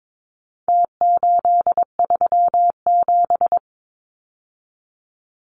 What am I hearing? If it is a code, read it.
Morse code, heard as a keyed tone, about 22 wpm, 700 Hz. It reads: T837
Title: Numbers - Contest Style - 4-Digit 22wpm